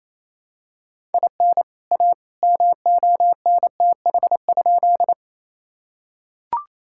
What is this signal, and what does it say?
Morse code, heard as a keyed tone, about 28 wpm, 700 Hz. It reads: IN A MONTH?
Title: Sentences from Top 1000 Words - Review Part 5 - 28wpm